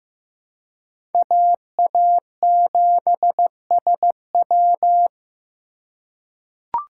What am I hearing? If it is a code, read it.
Morse code, heard as a keyed tone, about 15 wpm, 700 Hz. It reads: AA7SW
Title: USA Call Signs - 5 Characters - 3x Character Spacing - 15wpm